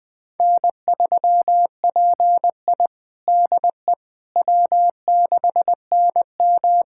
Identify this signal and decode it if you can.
Morse code, heard as a keyed tone, about 21 wpm, 700 Hz. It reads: N3PI DE W6NM
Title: USA Callsign DE Callsign - 4 Characters - 20wpm